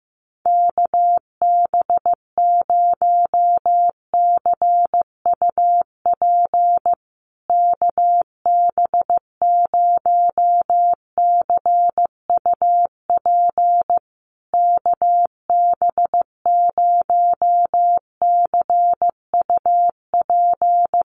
Morse code, heard as a keyed tone, about 15 wpm, 700 Hz. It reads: KB0CUP KB0CUP KB0CUP
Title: USA Call Signs Repeated 3x 15wpm